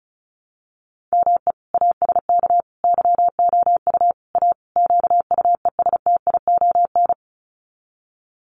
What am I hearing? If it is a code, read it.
Morse code, heard as a keyed tone, about 34 wpm, 700 Hz. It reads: ME ASK YOU A QUESTION
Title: Sets of 5 Words Encore2 35wpm